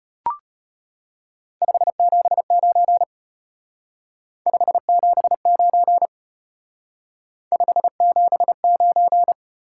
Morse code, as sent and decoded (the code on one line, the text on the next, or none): ..... --... ----. / ..... --... ----. / ..... --... ----.
579 579 579